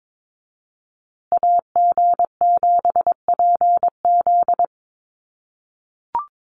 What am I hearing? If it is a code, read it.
Morse code, heard as a keyed tone, about 22 wpm, 700 Hz. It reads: AG7PZ
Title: USA Call Signs Repeated 2x 22wpm